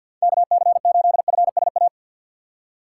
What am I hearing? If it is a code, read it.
Morse code, heard as a keyed tone, about 47 wpm, 700 Hz. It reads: KC8FSA
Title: USA Call Signs - 6 Characters - 4x Character Spacing - 50wpm